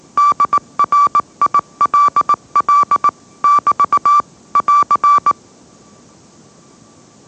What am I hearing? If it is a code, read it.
Morse code, heard as a keyed tone, about 18 wpm, 1160 Hz. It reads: DRILL <BT> <AR>
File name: mrx / mrx081 20wpm